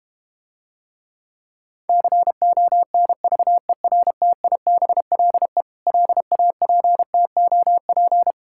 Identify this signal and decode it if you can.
Morse code, heard as a keyed tone, about 32 wpm, 700 Hz. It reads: CONVERTIBLE LAPTOP